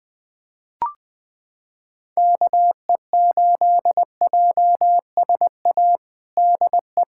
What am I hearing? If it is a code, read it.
Morse code, heard as a keyed tone, about 20 wpm, 700 Hz. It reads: KE8JSA DE
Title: USA Callsign DE Callsign - 6 Characters - 20wpm